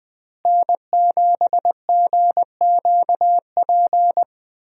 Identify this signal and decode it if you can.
Morse code, heard as a keyed tone, about 21 wpm, 700 Hz. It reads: N7GQP
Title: USA Call Signs - 5 Characters - Repeated 3x - 20wpm